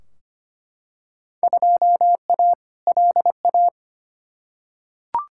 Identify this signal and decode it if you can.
Morse code, heard as a keyed tone, about 25 wpm, 700 Hz. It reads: 2A LA
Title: Field Day Exchanges Encore - 25wpm